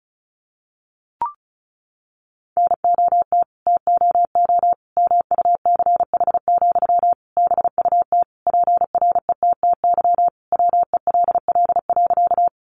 Morse code, heard as a keyed tone, about 34 wpm, 700 Hz. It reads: NOT TOO MUCH, BUT PRETTY WELL.
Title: Sentences from Top 1000 Words 35wpm